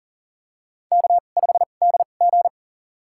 Morse code, heard as a keyed tone, about 43 wpm, spaced as for 30 wpm, 690 Hz. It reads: K5DG
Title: USA Call Signs - 4 Characters - 2x Character Spacing - 40wpm